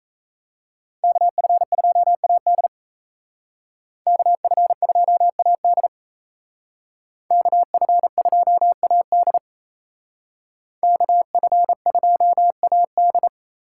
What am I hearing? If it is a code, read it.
Morse code, heard as a keyed tone, about 36 wpm, 700 Hz. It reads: KF2AD KF2AD KF2AD KF2AD